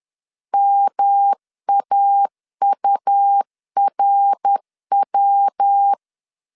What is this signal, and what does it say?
Morse code, heard as a keyed tone, about 10 wpm, 800 Hz. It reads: MAURW
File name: CW / DET / 1 3 1500C S C 10wpm 3335